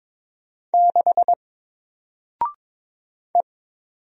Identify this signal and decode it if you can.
Morse code, heard as a keyed tone, about 22 wpm, 700 Hz. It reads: 6 E
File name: Single-Letter-Number-22wpm